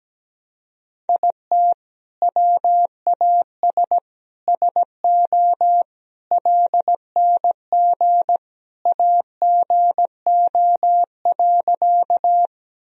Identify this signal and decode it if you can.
Morse code, heard as a keyed tone, about 17 wpm, 700 Hz. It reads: IT WAS SO LNG AGO.
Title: Sentences from Top 500 Words and Abbreviations 17wpm